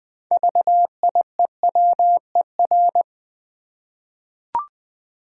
Morse code, heard as a keyed tone, about 20 wpm, 700 Hz. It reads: VIEWER
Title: 6 Letter Words - First 3 Letters Given - 20wpm